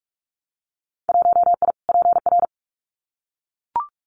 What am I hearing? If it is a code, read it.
Morse code, heard as a keyed tone, about 44 wpm, 700 Hz. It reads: 1I PR
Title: Winter Field Day Exchanges Encore 45wpm